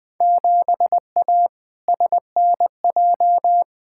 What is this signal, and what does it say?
Morse code, heard as a keyed tone, about 21 wpm, 700 Hz. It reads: 7A SNJ
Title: Field Day Exchanges Encore - 20wpm